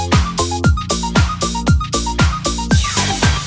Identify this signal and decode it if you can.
There is no code.